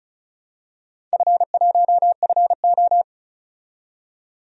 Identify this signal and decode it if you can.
Morse code, heard as a keyed tone, about 34 wpm, 700 Hz. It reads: F1FO
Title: DXCC-100 Call Signs - 4 Characters - 35wpm